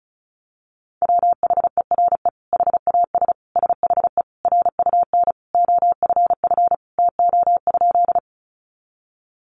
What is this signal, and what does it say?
Morse code, heard as a keyed tone, about 34 wpm, 700 Hz. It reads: WHERE HAS SHE RUN OFF TO?
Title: Sentences from Top 300 Words 35wpm